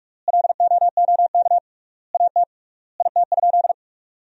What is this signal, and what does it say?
Morse code, heard as a keyed tone, about 46 wpm, 700 Hz. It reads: LOOK AT IT?
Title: 3x Word-Spacing - Sentences from Top 100 Words - 45wpm